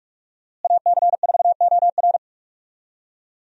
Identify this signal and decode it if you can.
Morse code, heard as a keyed tone, about 46 wpm, 700 Hz. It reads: AC4OR
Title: USA Call Signs - 5 Characters - Repeated 3x - 45wpm